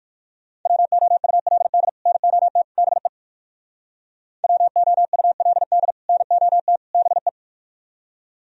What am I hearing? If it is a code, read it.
Morse code, heard as a keyed tone, about 58 wpm, 700 Hz. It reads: WOULD NOT BE WOULD NOT BE